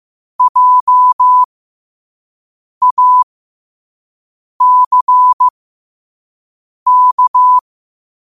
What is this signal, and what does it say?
Morse code, heard as a keyed tone, about 15 wpm, spaced as for 5 wpm, 1000 Hz. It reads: JACK
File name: qotdfiles / 5wpm